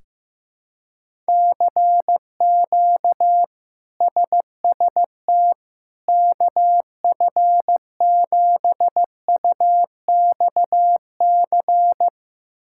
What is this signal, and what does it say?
Morse code, heard as a keyed tone, about 15 wpm, 700 Hz. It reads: CQ SST KF7UXC
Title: K1USN SST QSO Practice - 15wpm